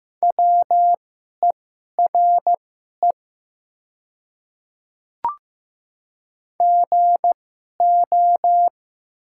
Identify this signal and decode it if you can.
Morse code, heard as a keyed tone, about 15 wpm, spaced as for 11 wpm, 700 Hz. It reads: WERE GO
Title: 2x Character Spacing - Top 100 Words - 15wpm